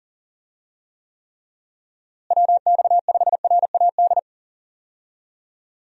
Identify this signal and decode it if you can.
Morse code, heard as a keyed tone, about 40 wpm, 700 Hz. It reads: WX5RAD
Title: USA Call Signs - 6 Characters - Repeated 3x - 40wpm